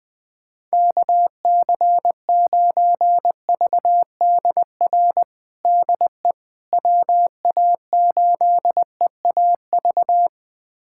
Morse code, heard as a keyed tone, about 20 wpm, 700 Hz. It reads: KC9VDR DE WA8EAV
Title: USA Callsign DE Callsign 20wpm